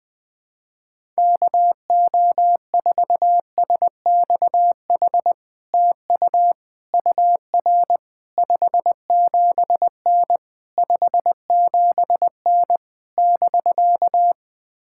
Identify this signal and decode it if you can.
Morse code, heard as a keyed tone, about 20 wpm, 700 Hz. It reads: KO4SXH TU UR 57N 57N <BK>
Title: POTA QSO Practice - 20wpm